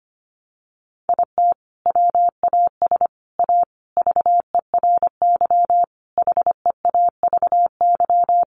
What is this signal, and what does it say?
Morse code, heard as a keyed tone, about 25 wpm, 700 Hz. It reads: IT WAS A VERY HEAVY
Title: Sentences from Top 1100 Words 25wpm